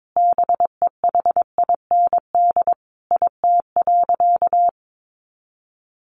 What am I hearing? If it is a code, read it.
Morse code, heard as a keyed tone, about 22 wpm, 700 Hz. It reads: BEHIND IT.
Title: Sentences from Top 500 Words 22wpm